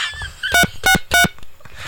DTMF, heard as telephone touch tones, tones of about 100 ms. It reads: 363